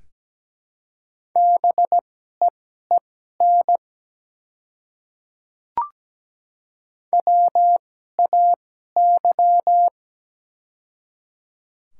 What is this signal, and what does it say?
Morse code, heard as a keyed tone, about 17 wpm, spaced as for 12 wpm, 700 Hz. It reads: BEEN WAY
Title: 2x Character Spacing - Top 200 Words - 17wpm